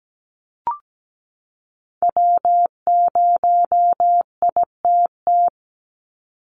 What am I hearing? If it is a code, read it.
Morse code, heard as a keyed tone, about 17 wpm, 700 Hz. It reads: W0ITT
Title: Call Signs Repeated 3x - 17wpm